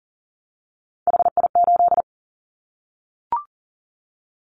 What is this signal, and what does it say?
Morse code, heard as a keyed tone, about 40 wpm, 700 Hz. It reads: HI8
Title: Callsign Prefixes for Top 100 DXCC - 40wpm